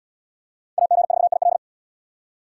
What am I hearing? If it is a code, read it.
Morse code, heard as a keyed tone, about 67 wpm, 700 Hz. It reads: AG4IL